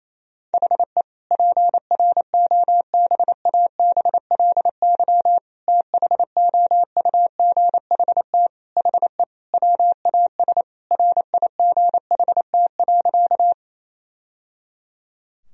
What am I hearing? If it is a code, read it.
Morse code, heard as a keyed tone, about 28 wpm, 700 Hz. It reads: HE PROBABLY THOUGHT HE WAS RIGHT.
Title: Sentences from Top 1000 Words - Review Part 4 - 28wpm